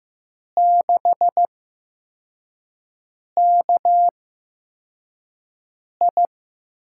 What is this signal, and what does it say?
Morse code, heard as a keyed tone, about 15 wpm, spaced as for 4 wpm, 700 Hz. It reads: 6KI
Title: USA Call Signs - 8x Character Spacing - 15wpm